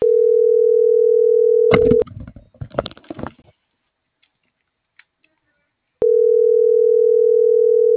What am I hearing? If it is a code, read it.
Ringback tone (North America), 440 + 480 Hz.